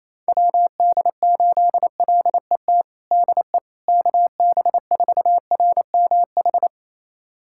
Morse code, heard as a keyed tone, about 28 wpm, 700 Hz. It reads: WD8LET DE KB4RMH
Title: USA Callsign DE Callsign 28wpm